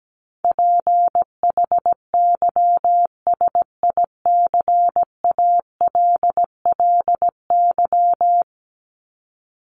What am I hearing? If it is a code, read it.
Morse code, heard as a keyed tone, about 17 wpm, 700 Hz. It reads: PHYSICALLY
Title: Top 1501-1750 Words 17wpm